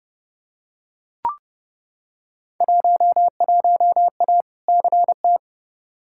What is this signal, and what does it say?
Morse code, heard as a keyed tone, about 29 wpm, 700 Hz. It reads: 11A CT
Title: Field Day Exchanges Encore - 30wpm